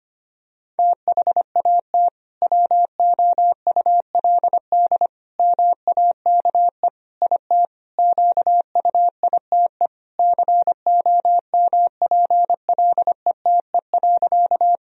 Morse code, heard as a keyed tone, about 25 wpm, 700 Hz. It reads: THAT WOULD MAKE IT QUITE COMPLETE.
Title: Sentences from Top 1000 Words - Review Part 2 - 25wpm